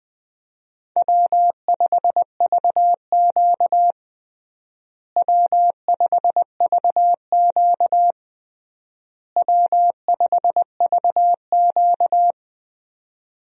Morse code, heard as a keyed tone, about 20 wpm, 700 Hz. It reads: W5VQ W5VQ W5VQ
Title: USA Call Signs - 4 Characters - Repeated 3x - 20wpm